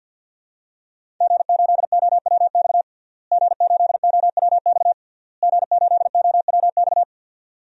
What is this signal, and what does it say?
Morse code, heard as a keyed tone, about 52 wpm, 700 Hz. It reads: G8OWX G8OWX G8OWX
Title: DXCC-100 Call Signs - 5 Characters - Repeated 3x - 50wpm